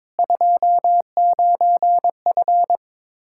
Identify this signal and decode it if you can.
Morse code, heard as a keyed tone, about 22 wpm, 700 Hz. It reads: 29F
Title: Practice QSO - Exchange 2 - Info - 22wpm